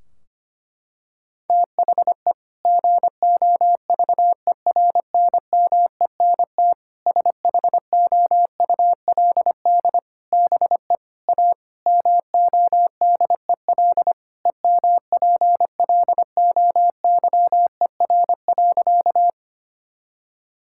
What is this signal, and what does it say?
Morse code, heard as a keyed tone, about 25 wpm, 700 Hz. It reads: THE GOVERNMENT SHOULD BE A MODEL EMPLOYER.
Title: Sentences from Top 1600 Words 25wpm